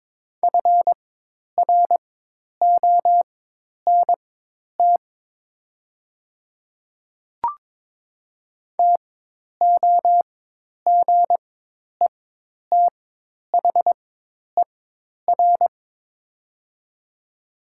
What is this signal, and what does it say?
Morse code, heard as a keyed tone, about 22 wpm, spaced as for 10 wpm, 700 Hz. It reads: FRONT TOGETHER
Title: 4x Character Spacing - Top 500 Words - 22wpm